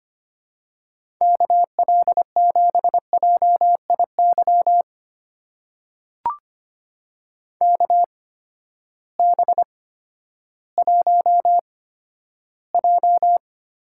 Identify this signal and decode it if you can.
Morse code, heard as a keyed tone, about 25 wpm, spaced as for 13 wpm, 700 Hz. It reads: KL7JIY KB1J
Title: USA Call Signs - 6 Characters - 8x Character Spacing - 25wpm